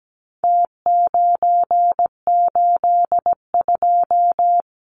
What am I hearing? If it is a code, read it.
Morse code, heard as a keyed tone, about 17 wpm, 700 Hz. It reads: T982
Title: Numbers - Contest Style - 4-Digit 17wpm